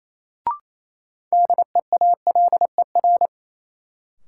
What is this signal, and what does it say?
Morse code, heard as a keyed tone, about 28 wpm, 700 Hz. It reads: DEALER
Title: Sentences from Top 1100 Words - New Word Drill Alt - 28wpm